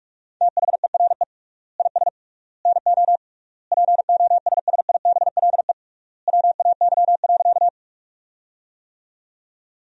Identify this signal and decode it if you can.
Morse code, heard as a keyed tone, about 46 wpm, 700 Hz. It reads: THERE IS NO POSSIBLE WAY.
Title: 3x Word-Spacing - Sentences from Top 500 Words - 45wpm